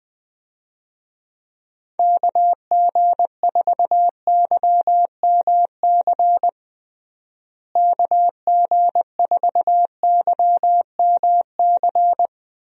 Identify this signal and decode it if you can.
Morse code, heard as a keyed tone, about 20 wpm, 700 Hz. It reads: KG4YMC KG4YMC
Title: USA Call Signs - 6 Characters - Repeated 2x - 20wpm